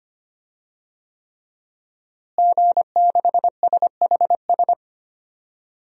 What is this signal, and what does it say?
Morse code, heard as a keyed tone, about 25 wpm, 700 Hz. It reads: G6SHS
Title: DXCC-100 Call Signs - 5 Characters - Repeated 3x - 25wpm